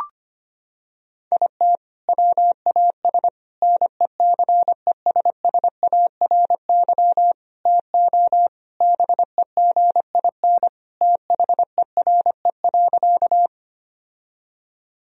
Morse code, heard as a keyed tone, about 25 wpm, 700 Hz. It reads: IT WAS NECESSARY TO BEGIN THERE.
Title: Sentences from Top 1000 Words - Review Part 1 - 25wpm